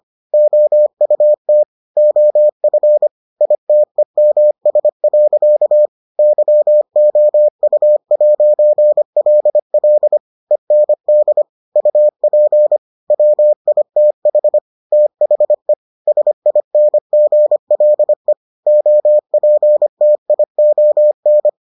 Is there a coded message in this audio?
Yes: Morse code from 0.3 to 21.5 s, heard as a keyed tone, about 25 wpm, 600 Hz. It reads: OUT OF ITEMS. YOU'LL END UP WITH THE SINGLE OPTION